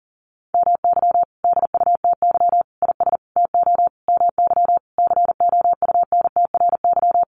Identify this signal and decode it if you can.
Morse code, heard as a keyed tone, about 40 wpm, 700 Hz. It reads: MY DUTY IS TO MY COUNTRY
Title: Sentences from Top 1500 Words 40wpm